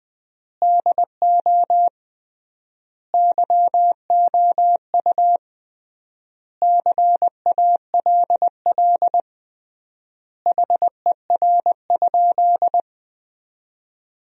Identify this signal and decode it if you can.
Morse code, heard as a keyed tone, about 20 wpm, 700 Hz. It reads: DO YOU CALL HER?
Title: 3x Word-Spacing - Sentences from Top 100 Words - 20wpm